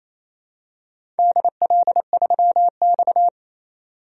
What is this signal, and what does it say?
Morse code, heard as a keyed tone, about 28 wpm, 700 Hz. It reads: DL3X
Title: DXCC-100 Call Signs - 4 Characters - 28wpm